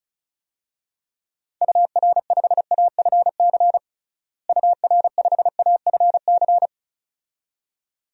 Morse code, heard as a keyed tone, about 36 wpm, 700 Hz. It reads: UR5AFC UR5AFC
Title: DXCC-100 Call Signs - 6 Characters - Repeated 2x - 35wpm